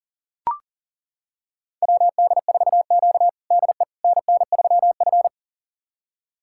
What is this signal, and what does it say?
Morse code, heard as a keyed tone, about 40 wpm, 690 Hz. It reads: WD4Q DE NN3F
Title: USA Callsign DE Callsign - 4 Characters - 40wpm